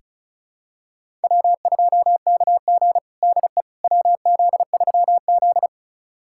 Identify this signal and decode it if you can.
Morse code, heard as a keyed tone, about 36 wpm, 700 Hz. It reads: W2KG DE WZ3Z
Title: USA Callsign DE Callsign - 4 Characters - 35wpm